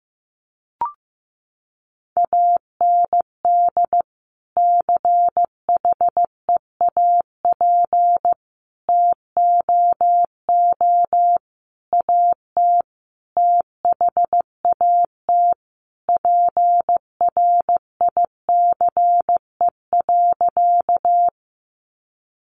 Morse code, heard as a keyed tone, about 15 wpm, 700 Hz. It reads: AND CHEAP TOO AT THAT PRICE.
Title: Sentences from Top 1200 Words 15wpm